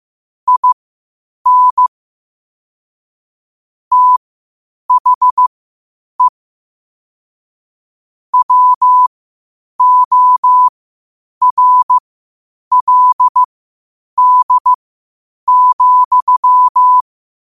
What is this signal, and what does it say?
Morse code, heard as a keyed tone, about 15 wpm, spaced as for 8 wpm, 1000 Hz. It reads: IN THE WORLD,
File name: qotdfiles / 8wpm